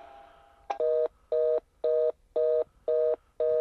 Reorder tone (North America), 480 + 620 Hz.
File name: vz pots ld carrier access code 518-735